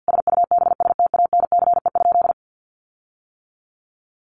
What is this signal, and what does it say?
Morse code, heard as a keyed tone, about 50 wpm, 700 Hz. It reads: SUBSTANCE?
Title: Sentences from Top 1500 Words 50wpm